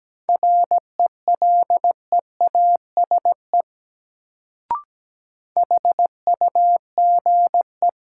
Morse code, heard as a keyed tone, about 17 wpm, 700 Hz. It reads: RELEASE HUGE
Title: Sentences from Top 1100 Words - New Word Drill - 17wpm